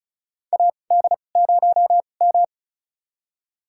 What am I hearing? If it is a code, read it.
Morse code, heard as a keyed tone, about 36 wpm, spaced as for 25 wpm, 700 Hz. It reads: AD0M
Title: USA Call Signs - 2x Character Spacing - 35wpm